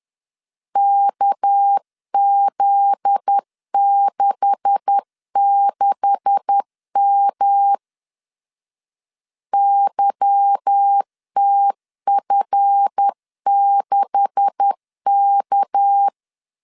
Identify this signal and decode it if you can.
Morse code, heard as a keyed tone, about 10 wpm, 800 Hz. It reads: KZ66M YTF6K
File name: LEZIONI / 9 286 10wpm 2735